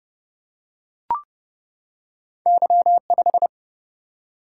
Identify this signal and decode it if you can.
Morse code, heard as a keyed tone, about 30 wpm, 700 Hz. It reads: Y5
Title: Single-Letter Single-Number 30wpm